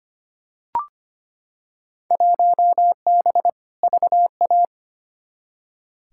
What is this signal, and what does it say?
Morse code, heard as a keyed tone, about 25 wpm, 700 Hz. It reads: 1B VA
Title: Field Day Exchanges Encore - 25wpm